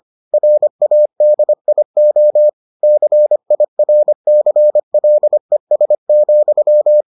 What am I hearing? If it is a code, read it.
Morse code, heard as a keyed tone, about 25 wpm, 600 Hz. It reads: RADIO CIRCLES,